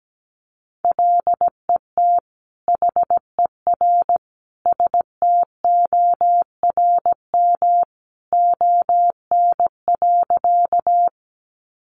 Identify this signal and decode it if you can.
Morse code, heard as a keyed tone, about 17 wpm, 700 Hz. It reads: LET HER STORM ON.
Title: Sentences from Top 1500 Words 17wpm